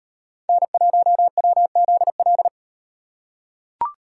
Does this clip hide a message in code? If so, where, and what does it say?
Yes: Morse code from 0.5 to 2.5 s, heard as a keyed tone, about 38 wpm, 700 Hz. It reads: N1WZL